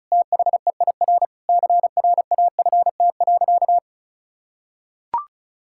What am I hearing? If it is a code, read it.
Morse code, heard as a keyed tone, about 36 wpm, 700 Hz. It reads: THEIR CRAFT.
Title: Sentences from Top 1000 Words - Review Part 6 - 35wpm